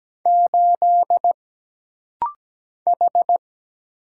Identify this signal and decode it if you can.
Morse code, heard as a keyed tone, about 17 wpm, 700 Hz. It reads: 8 H
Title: Single Letter-Number - Rapid-Fire 17wpm